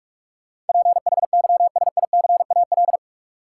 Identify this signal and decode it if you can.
Morse code, heard as a keyed tone, about 46 wpm, 700 Hz. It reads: PHYSICAL
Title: Sentences from Top 1000 Words - Review Part 4 - New Word Drill - 45wpm